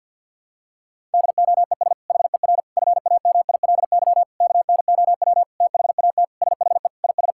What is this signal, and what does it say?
Morse code, heard as a keyed tone, about 52 wpm, 700 Hz. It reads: DOES HER FAMILY KNOW THAT SHE IS
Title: Sentences from Top 1000 Words - Review Part 6 - 50wpm